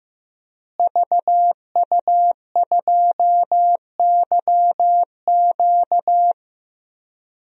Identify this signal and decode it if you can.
Morse code, heard as a keyed tone, about 15 wpm, 700 Hz. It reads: VU2YQ
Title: DXCC-100 Call Signs - 5 Characters - Repeated 3x - 15wpm